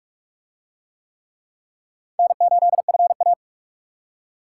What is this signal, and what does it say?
Morse code, heard as a keyed tone, about 40 wpm, 700 Hz. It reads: N8FA